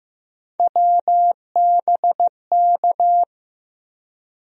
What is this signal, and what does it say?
Morse code, heard as a keyed tone, about 15 wpm, 700 Hz. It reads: WBK
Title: USA Call Sign Postfixes - 3 Letters 15wpm